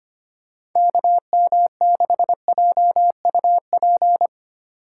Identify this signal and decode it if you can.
Morse code, heard as a keyed tone, about 25 wpm, 700 Hz. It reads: KM6JUP
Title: USA Call Signs - 6 Characters - 3x Character Spacing - 25wpm